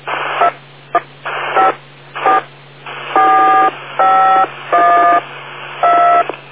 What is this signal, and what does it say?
DTMF, heard as telephone touch tones, tones of about 190 ms. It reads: 2154 451 2